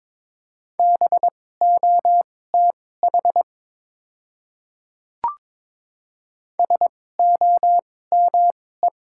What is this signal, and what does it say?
Morse code, heard as a keyed tone, about 21 wpm, spaced as for 16 wpm, 700 Hz. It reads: BOTH SOME
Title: 2x Character Spacing - Top 300 Words - 22wpm